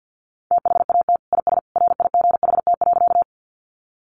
Morse code, heard as a keyed tone, about 50 wpm, 700 Hz. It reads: THAT IS RIGHT.
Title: Sentences from Top 200 Words 50wpm